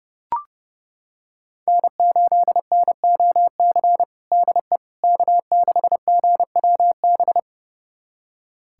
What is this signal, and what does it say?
Morse code, heard as a keyed tone, about 32 wpm, 700 Hz. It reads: N8NOC DE K6GWB
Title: USA Callsign DE Callsign - 5 Characters - 30wpm